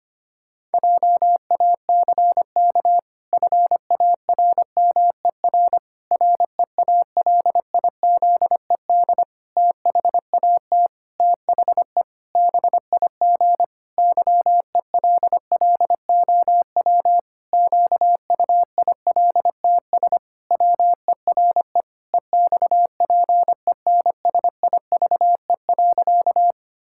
Morse code, heard as a keyed tone, about 25 wpm, 700 Hz. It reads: JACK FARMER REALIZED THAT THE BIG YELLOW QUILTS WERE EXPENSIVE.
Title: Pangram Challenge v2 - 25wpm